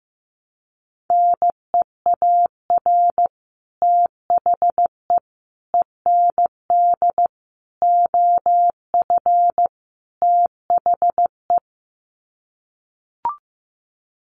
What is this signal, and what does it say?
Morse code, heard as a keyed tone, about 15 wpm, 700 Hz. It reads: NEAR THE END OF THE
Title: Sets of 5 Words Encore 15wpm